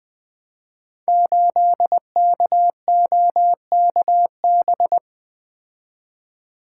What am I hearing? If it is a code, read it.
Morse code, heard as a keyed tone, about 20 wpm, 700 Hz. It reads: 8KOKB
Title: Code Groups - 5 Letter:Numbers - 20wpm